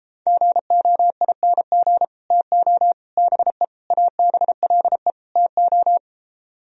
Morse code, heard as a keyed tone, about 33 wpm, 700 Hz. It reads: GOING TO BE ABLE TO